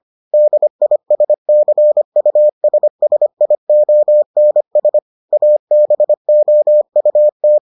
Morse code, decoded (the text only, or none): DISCUSSIONS ABOUT